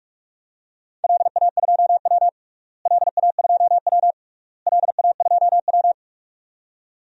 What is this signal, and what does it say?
Morse code, heard as a keyed tone, about 42 wpm, 700 Hz. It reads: LA2W LA2W LA2W
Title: DXCC-100 Call Signs - 4 Characters - Repeated 3x - 45wpm